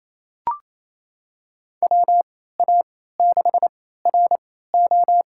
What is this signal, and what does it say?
Morse code, heard as a keyed tone, about 27 wpm, spaced as for 16 wpm, 700 Hz. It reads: WA6RO
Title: USA Call Signs - 3x Character Spacing - 28wpm